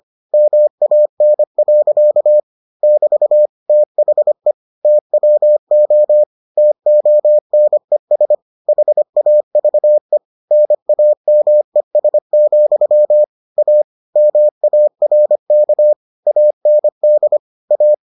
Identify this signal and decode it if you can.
Morse code, heard as a keyed tone, about 25 wpm, 600 Hz. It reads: MAN. <BT> THE TWO TONES HAVE NAMES, A MARK AND A